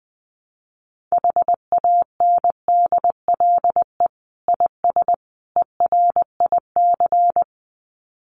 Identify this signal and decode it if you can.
Morse code, heard as a keyed tone, about 20 wpm, 700 Hz. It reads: HANDLE IS ERIC
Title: Common QSO Phrases 20wpm